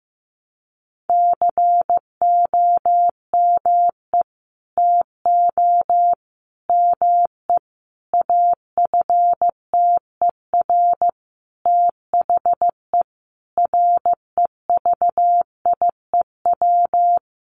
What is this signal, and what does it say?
Morse code, heard as a keyed tone, about 15 wpm, 700 Hz. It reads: COME TO ME AFTER THE REVIEW
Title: Sentences from Top 900 Words 15wpm